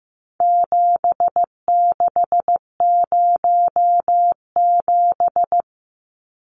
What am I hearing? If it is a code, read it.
Morse code, heard as a keyed tone, about 15 wpm, 700 Hz. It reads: 7607
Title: Numbers - 4-Digit 15wpm